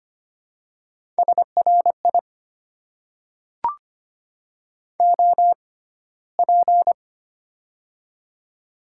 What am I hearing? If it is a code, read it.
Morse code, heard as a keyed tone, about 25 wpm, spaced as for 15 wpm, 700 Hz. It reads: SRI OP
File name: Top 90 QSO Elements - 6x Character Spacing - 25wpm